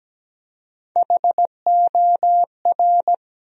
Morse code, heard as a keyed tone, about 17 wpm, 700 Hz. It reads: HOR
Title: USA Call Sign Postfixes - 3 Letters 17wpm